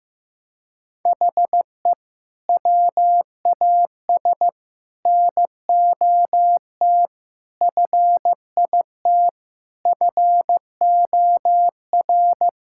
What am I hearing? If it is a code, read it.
Morse code, heard as a keyed tone, about 15 wpm, 690 Hz. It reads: HE WAS NOT FIT FOR
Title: Sentences from Top 1000 Words - Review Part 5 - 15wpm